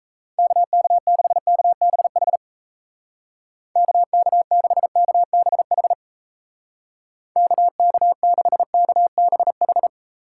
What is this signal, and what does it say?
Morse code, heard as a keyed tone, about 36 wpm, 700 Hz. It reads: KK6KBH KK6KBH KK6KBH